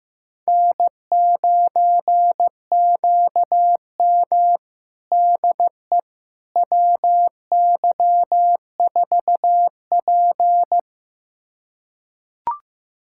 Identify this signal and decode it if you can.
Morse code, heard as a keyed tone, about 15 wpm, 700 Hz. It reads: N9QM DE WY4P
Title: USA Callsign DE Callsign - 4 Characters - 15wpm